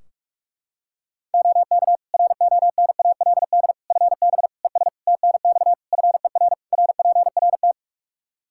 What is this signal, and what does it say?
Morse code, heard as a keyed tone, about 46 wpm, 700 Hz. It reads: OK RONALD FB ES TNX FER RPRT
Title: Practice QSO - Exchange 2 - Info - 45wpm